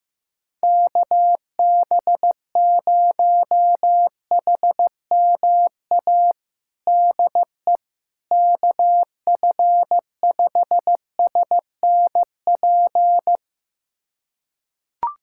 Morse code, heard as a keyed tone, about 15 wpm, 700 Hz. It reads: KB0HMA DE KF5SNP
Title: USA Callsign DE Callsign - 6 Characters - 15wpm